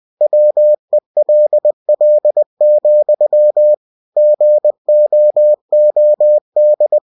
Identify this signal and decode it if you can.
Morse code, heard as a keyed tone, about 20 wpm, 600 Hz. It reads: WELL, GOOD
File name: cw-books / winnie-the-pooh / winnie-20wpm-page-0038